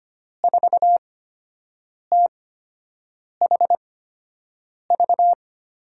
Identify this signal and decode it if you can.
Morse code, heard as a keyed tone, about 25 wpm, spaced as for 7 wpm, 700 Hz. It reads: 4THV
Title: USA Call Signs - 6 Characters - 8x Character Spacing - 25wpm